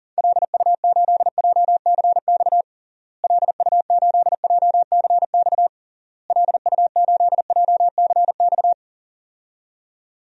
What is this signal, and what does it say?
Morse code, heard as a keyed tone, about 40 wpm, 700 Hz. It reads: LU8JCX LU8JCX LU8JCX
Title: DXCC-100 Call Signs - 6 Characters - Repeated 3x - 40wpm